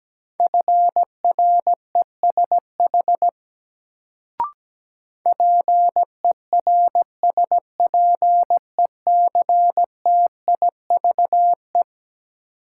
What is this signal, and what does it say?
Morse code, heard as a keyed tone, about 17 wpm, 700 Hz. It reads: FRESH PERSPECTIVE
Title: Sentences from Top 1200 Words - New Word Drill - 17wpm